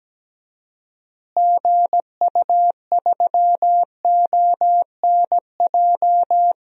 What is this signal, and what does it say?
Morse code, heard as a keyed tone, about 17 wpm, 700 Hz. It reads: GU3ONJ
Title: DXCC-100 Call Signs - 6 Characters - Repeated 2x - 17wpm